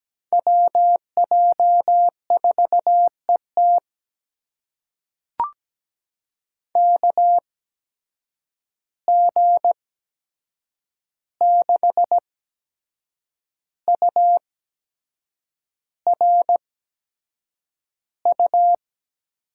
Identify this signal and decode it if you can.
Morse code, heard as a keyed tone, about 17 wpm, spaced as for 7 wpm, 700 Hz. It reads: WJ4ET KG6URU
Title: USA Call Signs - 8x Character Spacing - 17wpm